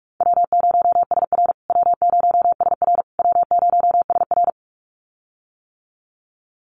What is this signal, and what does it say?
Morse code, heard as a keyed tone, about 44 wpm, 700 Hz. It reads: W0SR W0SR W0SR
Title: USA Call Signs Repeated 3x 45wpm